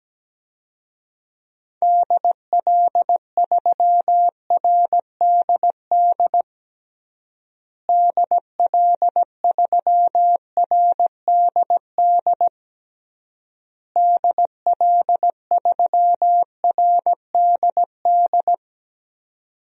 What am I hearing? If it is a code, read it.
Morse code, heard as a keyed tone, about 17 wpm, 700 Hz. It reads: DL3RDD DL3RDD DL3RDD
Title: DXCC-100 Call Signs - 6 Characters - Repeated 3x - 17wpm